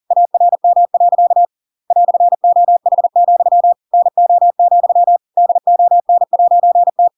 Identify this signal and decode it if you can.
Morse code, heard as a keyed tone, about 40 wpm, 700 Hz. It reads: ARM. "OH, NO, DON'T
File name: Pinocchio / Pinocchio ch7 40 40 ik1hge